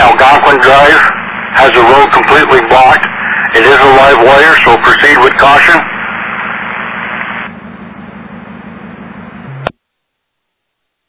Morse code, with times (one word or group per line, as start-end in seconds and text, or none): none